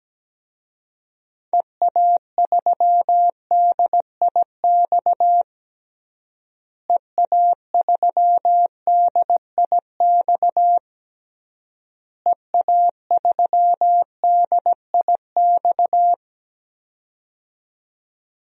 Morse code, heard as a keyed tone, about 17 wpm, 700 Hz. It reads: EA3DIX EA3DIX EA3DIX
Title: DXCC-100 Call Signs - 6 Characters - Repeated 3x - 17wpm